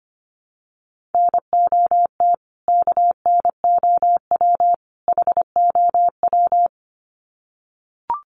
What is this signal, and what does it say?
Morse code, heard as a keyed tone, about 25 wpm, 700 Hz. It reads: NOT KNOW HOW
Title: Sets of 3 Words 25wpm